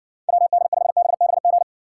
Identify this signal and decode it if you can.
Morse code, heard as a keyed tone, about 64 wpm, 700 Hz. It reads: WD5BBG